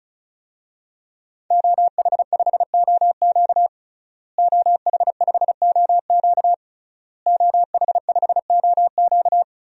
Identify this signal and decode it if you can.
Morse code, heard as a keyed tone, about 36 wpm, 700 Hz. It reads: OH5OQ OH5OQ OH5OQ
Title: DXCC-100 Call Signs - 5 Characters - Repeated 3x - 35wpm